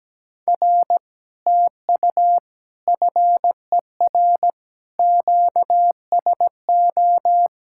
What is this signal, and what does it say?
Morse code, heard as a keyed tone, about 17 wpm, 700 Hz. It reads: R TU FER QSO
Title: Speed-Racing Common QSO Phrases Encore - 17wpm